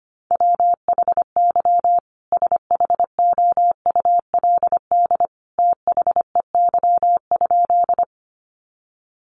Morse code, heard as a keyed tone, about 25 wpm, 700 Hz. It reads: WHY SHOULD THEY?
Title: Sentences from Top 300 Words 25wpm